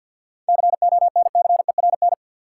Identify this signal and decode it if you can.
Morse code, heard as a keyed tone, about 51 wpm, 700 Hz. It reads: CONCERN
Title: Seven Letter Words - 50wpm